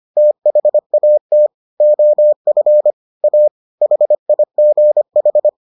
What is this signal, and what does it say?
Morse code, heard as a keyed tone, about 25 wpm, 600 Hz. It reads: THAT OF A HIGH